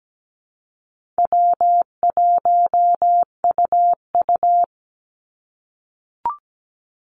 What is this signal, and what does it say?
Morse code, heard as a keyed tone, about 17 wpm, 700 Hz. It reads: W1UU
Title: Easy Call Signs - 17wpm